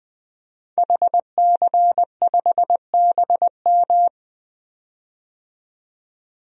Morse code, heard as a keyed tone, about 20 wpm, 700 Hz. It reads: HC5BM
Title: DXCC-100 Call Signs - 5 Characters - 20wpm